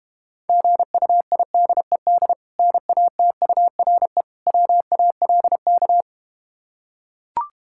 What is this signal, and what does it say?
Morse code, heard as a keyed tone, about 31 wpm, 700 Hz. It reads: GUIDED NATURE WALK
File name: practice / 25wpm / Difficult Words - Speed-Racing with Hints - 25wpm